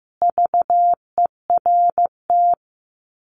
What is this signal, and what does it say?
Morse code, heard as a keyed tone, about 15 wpm, 700 Hz. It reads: VERT
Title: Common QSO Phrases 15wpm